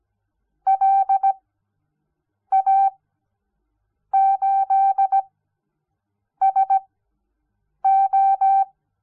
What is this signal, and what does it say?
Morse code, heard as a keyed tone, about 17 wpm, spaced as for 6 wpm, 780 Hz. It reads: LA8SO
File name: Onlinelehrgang / morsen / DJ4UF Morsen06